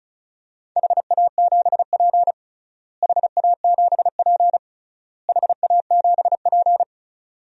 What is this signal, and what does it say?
Morse code, heard as a keyed tone, about 34 wpm, 700 Hz. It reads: HA7P HA7P HA7P
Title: DXCC-100 Call Signs - 4 Characters - Repeated 3x - 35wpm